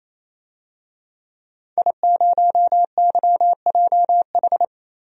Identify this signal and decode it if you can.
Morse code, heard as a keyed tone, about 28 wpm, 700 Hz. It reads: I0YJH
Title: DXCC-100 Call Signs - 5 Characters - 28wpm